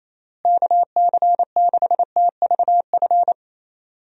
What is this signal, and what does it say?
Morse code, heard as a keyed tone, about 28 wpm, 700 Hz. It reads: KC6TVF
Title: USA Call Signs - 6 Characters - Repeated 2x - 28wpm